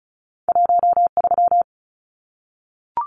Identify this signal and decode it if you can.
Morse code, heard as a keyed tone, about 34 wpm, 700 Hz. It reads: 13
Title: Numbers - 2-Digit 35wpm